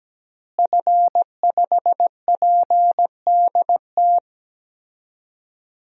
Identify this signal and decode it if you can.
Morse code, heard as a keyed tone, about 17 wpm, 700 Hz. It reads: F5PDT
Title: DXCC-100 Call Signs - 5 Characters - Repeated 2x - 17wpm